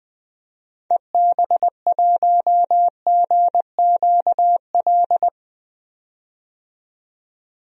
Morse code, heard as a keyed tone, about 20 wpm, 700 Hz. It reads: EB1GQL
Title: DXCC-100 Call Signs - 6 Characters - 20wpm